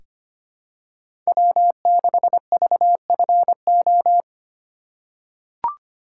Morse code, heard as a keyed tone, about 25 wpm, 700 Hz. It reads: W6VFO
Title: USA Call Signs - 5 Characters - 4x Character Spacing - 25wpm